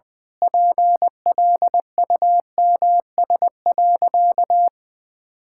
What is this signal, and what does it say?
Morse code, heard as a keyed tone, about 21 wpm, 700 Hz. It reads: PLUMS.
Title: Pangram Challenge v2 - 20wpm